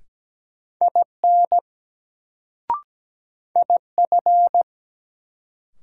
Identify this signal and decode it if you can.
Morse code, heard as a keyed tone, about 17 wpm, 700 Hz. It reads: IN IF
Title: Sentences from Top 100 Words - New Word Drill - 17wpm